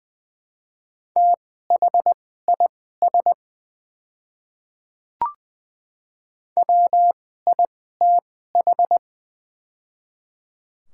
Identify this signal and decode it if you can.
Morse code, heard as a keyed tone, about 21 wpm, spaced as for 15 wpm, 700 Hz. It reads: THIS WITH
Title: 2x Character Spacing - Top 100 Words - 20wpm